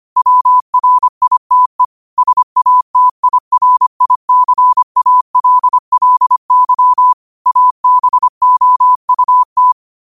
Morse code, heard as a keyed tone, about 25 wpm, 1000 Hz. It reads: WRITE SATIRICALLY ABOUT
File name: qotdfiles / 25wpm